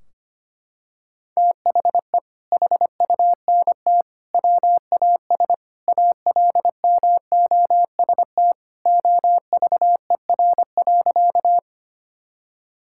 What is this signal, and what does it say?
Morse code, heard as a keyed tone, about 25 wpm, 700 Hz. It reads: THE HUNT WAS ALMOST OVER.
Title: Sentences from Top 1600 Words 25wpm